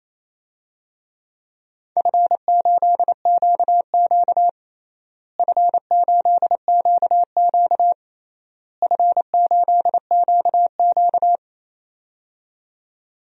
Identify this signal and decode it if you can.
Morse code, heard as a keyed tone, about 28 wpm, 700 Hz. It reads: F8QQ F8QQ F8QQ
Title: DXCC-100 Call Signs - 4 Characters - Repeated 3x - 28wpm